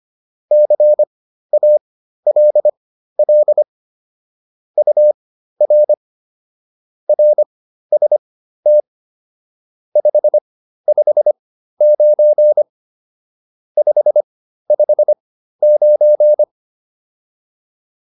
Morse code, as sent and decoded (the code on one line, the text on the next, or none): -.-. .- .-.. .-.. / ..- .-. / .-. ... - / ..... ..... ----. / ..... ..... ----.
CALL UR RST 559 559